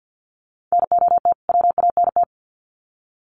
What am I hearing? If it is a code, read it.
Morse code, heard as a keyed tone, about 50 wpm, 700 Hz. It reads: NOT WANT
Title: Sets of 2 Words 50wpm